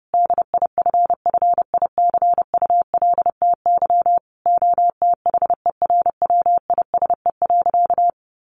Morse code, heard as a keyed tone, about 30 wpm, 700 Hz. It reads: DIFFICULTY OTHERWISE.
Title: Sentences from Top 1500 Words 30wpm